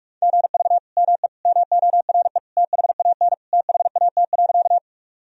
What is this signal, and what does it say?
Morse code, heard as a keyed tone, about 46 wpm, 700 Hz. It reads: GV ME MORE THAN THAT.
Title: Sentences from Top 200 Words and Abbreviations 45wpm